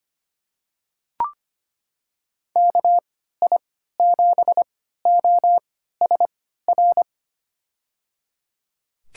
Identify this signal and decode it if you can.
Morse code, heard as a keyed tone, about 25 wpm, spaced as for 14 wpm, 700 Hz. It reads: KI7OSR
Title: USA Call Signs - 6 Characters - 3x Character Spacing - 25wpm